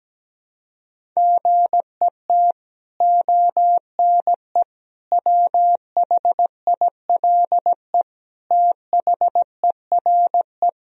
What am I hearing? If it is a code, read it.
Morse code, heard as a keyed tone, about 17 wpm, 700 Hz. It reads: GET ONE WHILE THERE
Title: Sentences from Top 1000 Words - Review Part 1 - 17wpm